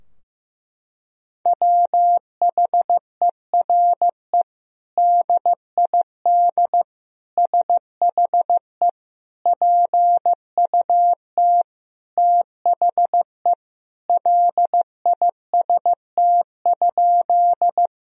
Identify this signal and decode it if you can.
Morse code, heard as a keyed tone, about 15 wpm, 700 Hz. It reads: WHERE DID SHE PUT THE LIST?
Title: Sentences from Top 1000 Words - Review Part 2 - 15wpm